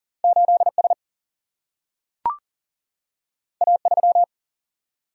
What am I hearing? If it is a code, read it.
Morse code, heard as a keyed tone, about 39 wpm, 700 Hz. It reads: 8 S A 3
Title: Single-Letter Single-Number 40wpm